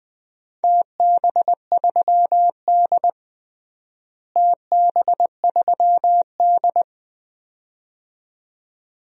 Morse code, heard as a keyed tone, about 20 wpm, 700 Hz. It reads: TB3D TB3D
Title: DXCC-100 Call Signs - 4 Characters - Repeated 2x - 20wpm